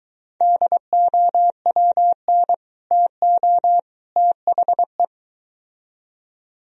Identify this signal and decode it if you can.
Morse code, heard as a keyed tone, about 23 wpm, 700 Hz. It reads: DOWN TO THE